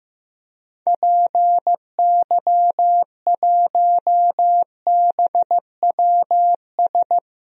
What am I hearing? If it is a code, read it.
Morse code, heard as a keyed tone, about 15 wpm, 700 Hz. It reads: PY1BWS
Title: DXCC-100 Call Signs - 6 Characters - 15wpm